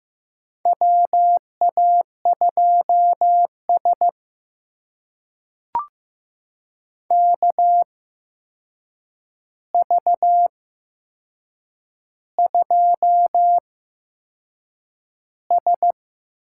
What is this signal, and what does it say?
Morse code, heard as a keyed tone, about 15 wpm, spaced as for 6 wpm, 700 Hz. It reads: WA2S KV2S
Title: USA Call Signs - 4 Characters - 8x Character Spacing - 15wpm